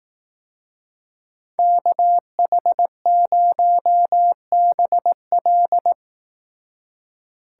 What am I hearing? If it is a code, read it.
Morse code, heard as a keyed tone, about 18 wpm, 700 Hz. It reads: KH0BL
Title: Speed Racing - USA 2x2 Call Signs 15wpm